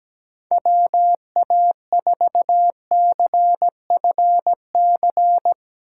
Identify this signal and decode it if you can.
Morse code, heard as a keyed tone, about 17 wpm, 700 Hz. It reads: WA4CFC
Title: USA Call Signs - 6 Characters - 4x Character Spacing - 17wpm